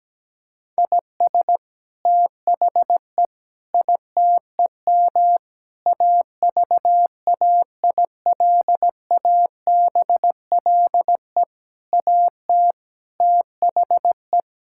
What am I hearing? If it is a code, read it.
Morse code, heard as a keyed tone, about 17 wpm, 700 Hz. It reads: IS THE ITEM AVAILABLE AT THE
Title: Sentences from Top 1000 Words - Review Part 6 - 17wpm